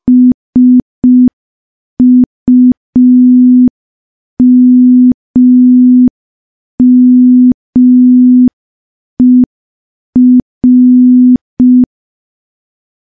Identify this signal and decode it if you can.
Morse code, heard as a keyed tone, about 5 wpm, 250 Hz. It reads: SUMMER
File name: morse / morse-training-05WPM